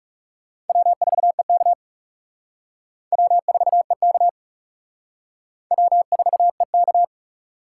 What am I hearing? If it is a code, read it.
Morse code, heard as a keyed tone, about 38 wpm, 700 Hz. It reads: W4EK W4EK W4EK